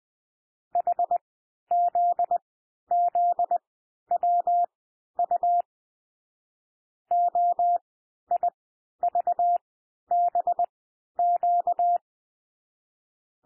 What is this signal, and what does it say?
Morse code, heard as a keyed tone, about 20 wpm, spaced as for 11 wpm, 700 Hz. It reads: HZZWU OIVBQ
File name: cwmp3 / L1 SOUND LETTERS 20 wpm